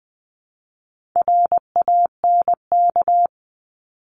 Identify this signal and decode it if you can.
Morse code, heard as a keyed tone, about 20 wpm, 700 Hz. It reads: RANK
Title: 260 Lesson 42e - Words 20-20wpm